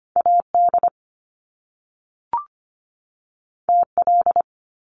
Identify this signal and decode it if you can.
Morse code, heard as a keyed tone, about 25 wpm, 700 Hz. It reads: AD TL
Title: Top Two Letter Combinations in Words 25wpm